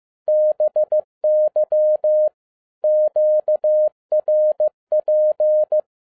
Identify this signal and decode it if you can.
Morse code, heard as a keyed tone, about 15 wpm, 610 Hz. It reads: BY QRP
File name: EXCITING QRP CW OPERATION 15wpm